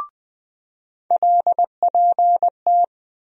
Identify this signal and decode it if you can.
Morse code, heard as a keyed tone, about 20 wpm, 700 Hz. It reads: LPT
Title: USA Call Sign Postfixes - 3 Letters 20wpm